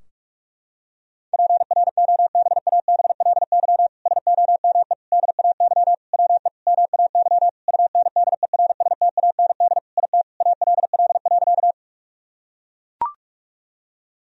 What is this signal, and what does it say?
Morse code, heard as a keyed tone, about 44 wpm, 700 Hz. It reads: PROBABLY SOME DAY WE MAY UNDERSTAND IT ALL.
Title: Sentences from Top 1000 Words - Review Part 4 - 45wpm